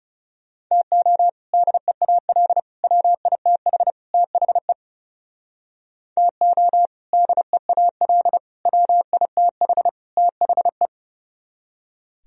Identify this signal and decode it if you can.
Morse code, heard as a keyed tone, about 38 wpm, 700 Hz. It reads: TO DEAL WITH THE TO DEAL WITH THE